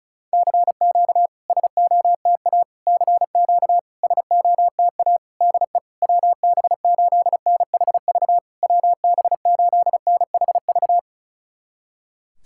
Morse code, heard as a keyed tone, about 36 wpm, 690 Hz. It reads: CQ SOTA CQ SOTA DE WB8NHV WB8NHV
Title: SOTA QSO Practice 35wpm